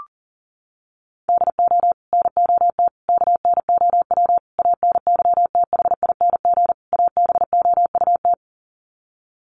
Morse code, heard as a keyed tone, about 40 wpm, 700 Hz. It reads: DO NOT KNOW ANYTHING ABOUT
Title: Sets of 5 Words 40wpm